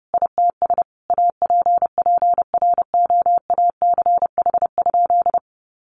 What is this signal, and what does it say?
Morse code, heard as a keyed tone, about 30 wpm, 700 Hz. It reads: ITS APPROACH?
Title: Sentences from Top 1500 Words 30wpm